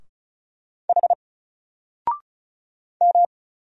Morse code, heard as a keyed tone, about 35 wpm, 700 Hz. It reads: H M
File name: Single-Letter-Number-35wpm